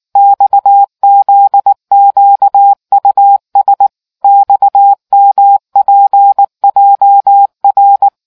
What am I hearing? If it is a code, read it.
Morse code, heard as a keyed tone, about 19 wpm, 800 Hz. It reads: XZQUS XMPJR